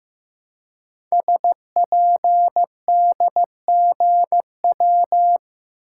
Morse code, heard as a keyed tone, about 15 wpm, 700 Hz. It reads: SPDGW
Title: Code Groups - 5 Letters - 15wpm